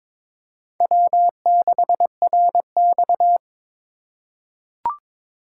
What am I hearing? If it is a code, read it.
Morse code, heard as a keyed tone, about 22 wpm, 700 Hz. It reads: W6RX
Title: USA Call Signs - 4 Characters - 4x Character Spacing - 22wpm